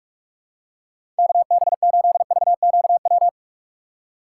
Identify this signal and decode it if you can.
Morse code, heard as a keyed tone, about 42 wpm, 700 Hz. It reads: KB8VQW
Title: USA Call Signs - 6 Characters - 2x Character Spacing - 45wpm